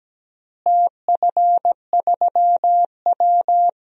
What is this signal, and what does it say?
Morse code, heard as a keyed tone, about 17 wpm, 700 Hz. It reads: TF3W
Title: DXCC-100 Call Signs - 4 Characters - Repeated 3x - 17wpm